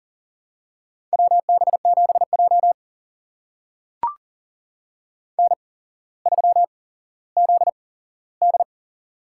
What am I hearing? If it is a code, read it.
Morse code, heard as a keyed tone, about 42 wpm, spaced as for 17 wpm, 700 Hz. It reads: WB7J N3ZD
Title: USA Call Signs - 4 Characters - 8x Character Spacing - 40wpm